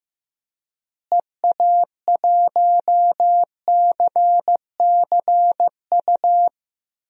Morse code, heard as a keyed tone, about 16 wpm, 700 Hz. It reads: EA1CCU
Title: DXCC-100 Call Signs - 6 Characters - Repeated 3x - 15wpm